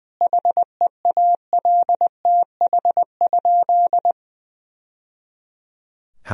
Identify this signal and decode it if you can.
Morse code, heard as a keyed tone, about 20 wpm, 700 Hz. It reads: HEALTH?
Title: Sentences from Top 1000 Words - Review Part 3 - 20wpm